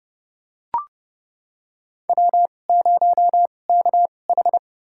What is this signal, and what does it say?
Morse code, heard as a keyed tone, about 30 wpm, spaced as for 21 wpm, 700 Hz. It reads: W0KH
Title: USA Call Signs - 2x Character Spacing - 30wpm